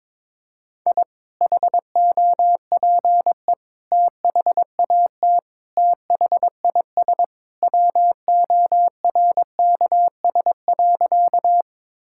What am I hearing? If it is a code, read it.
Morse code, heard as a keyed tone, about 22 wpm, 700 Hz. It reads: I HOPE THAT THIS WORKS.
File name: practice / 22wpm / Sentences with 2-Word Phrases - 22wpm